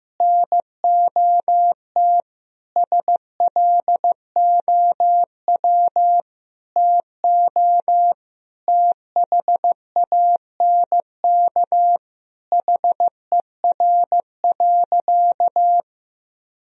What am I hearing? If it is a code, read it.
Morse code, heard as a keyed tone, about 15 wpm, 700 Hz. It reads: NOT SLOW TO THANK HER.
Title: Sentences from Top 1000 Words - Review Part 2 - 15wpm